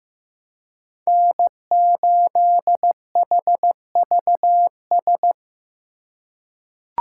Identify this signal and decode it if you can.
Morse code, heard as a keyed tone, about 15 wpm, 700 Hz. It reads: N8HVS
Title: USA Call Signs - 4x Character Spacing - 15wpm